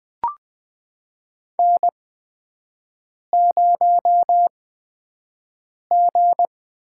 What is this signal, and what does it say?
Morse code, heard as a keyed tone, about 20 wpm, spaced as for 5 wpm, 700 Hz. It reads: N0G
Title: USA Call Signs - 4 Characters - 8x Character Spacing - 20wpm